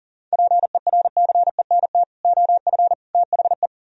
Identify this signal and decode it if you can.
Morse code, heard as a keyed tone, about 53 wpm, spaced as for 47 wpm, 700 Hz. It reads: PERCENT OF THE
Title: Speed-Racing - Sets of 3 Words 40wpm